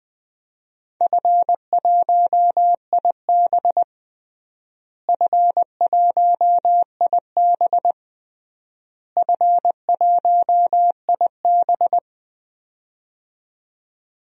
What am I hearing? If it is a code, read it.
Morse code, heard as a keyed tone, about 20 wpm, 700 Hz. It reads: F1IB F1IB F1IB
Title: DXCC-100 Call Signs - 4 Characters - Repeated 3x - 20wpm